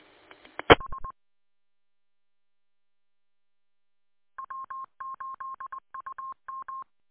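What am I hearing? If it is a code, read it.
Morse code, heard as a keyed tone, about 20 wpm, 1070 Hz. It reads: H W8UM
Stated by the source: Morse identification heard on W8UM's 145.230 MHz repeater audio archive.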